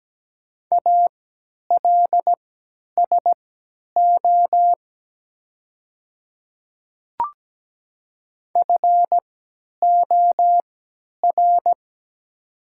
Morse code, heard as a keyed tone, about 17 wpm, spaced as for 10 wpm, 700 Hz. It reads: ALSO FOR
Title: 3x Character Spacing - Top 300 Words - 17wpm